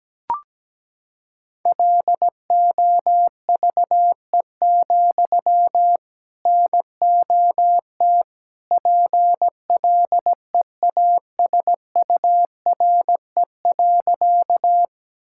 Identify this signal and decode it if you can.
Morse code, heard as a keyed tone, about 17 wpm, 700 Hz. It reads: LOVE, NOT PLEASURE.
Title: Sentences from Top 1600 Words 17wpm